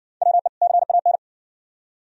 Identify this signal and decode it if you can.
Morse code, heard as a keyed tone, about 60 wpm, 700 Hz. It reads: WE CAN